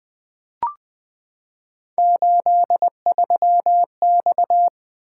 Morse code, heard as a keyed tone, about 20 wpm, 700 Hz. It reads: 83X
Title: Three Letter-Number Combinations - 20wpm